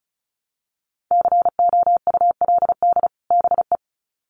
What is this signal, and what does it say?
Morse code, heard as a keyed tone, about 34 wpm, 700 Hz. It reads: COULD BE
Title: Sets of 2 Words 35wpm